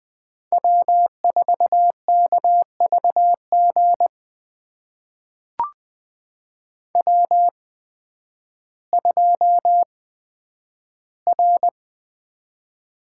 Morse code, heard as a keyed tone, about 20 wpm, spaced as for 11 wpm, 700 Hz. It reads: W4KVG W2R
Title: USA Call Signs - 8x Character Spacing - 20wpm